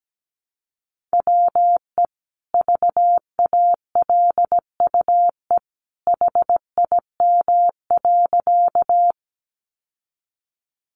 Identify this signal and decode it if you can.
Morse code, heard as a keyed tone, about 17 wpm, 700 Hz. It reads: WE VALUE HIM.
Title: Sentences from Top 600 Words 17wpm